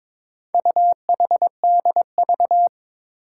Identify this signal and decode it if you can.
Morse code, heard as a keyed tone, about 22 wpm, 700 Hz. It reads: UHDV
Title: Four Challenging Letter-Number Combinations 22wpm